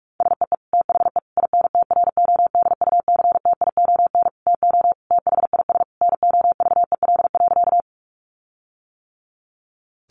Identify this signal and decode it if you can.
Morse code, heard as a keyed tone, about 46 wpm, 700 Hz. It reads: SEE THE INTRODUCTION TO THIS NOVEL.
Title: Sentences from Top 1300 Words 45wpm